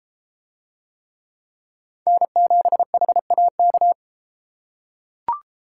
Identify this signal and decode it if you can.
Morse code, heard as a keyed tone, about 31 wpm, 700 Hz. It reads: N7HAK